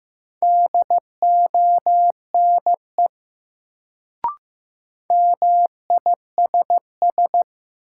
Morse code, heard as a keyed tone, about 15 wpm, 700 Hz. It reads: DONE MISS
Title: Sentences from Top 500 Words - New Word Drill - 15wpm